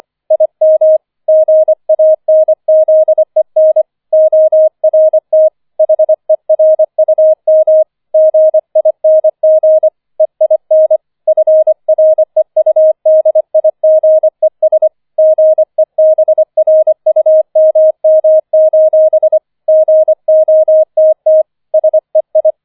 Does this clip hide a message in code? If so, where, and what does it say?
Yes: Morse code from 0.3 to 22.5 s, heard as a keyed tone, about 25 wpm, 630 Hz. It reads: IM GANZEN ORT HERUM GING EIN FREUDIGES GEBRUMM: GOTT SEI